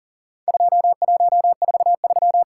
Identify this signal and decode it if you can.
Morse code, heard as a keyed tone, about 40 wpm, 700 Hz. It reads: 2143
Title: Worked All Europe DX Contest - QTC Exchange - 40wpm